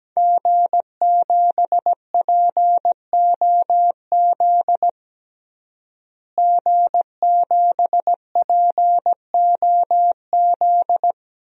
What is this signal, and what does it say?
Morse code, heard as a keyed tone, about 17 wpm, 700 Hz. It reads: G7POZ G7POZ
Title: DXCC-100 Call Signs - 5 Characters - Repeated 3x - 17wpm